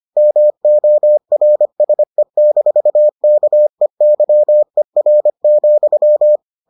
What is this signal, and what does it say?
Morse code, heard as a keyed tone, about 25 wpm, 600 Hz. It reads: MORSE-KEYER,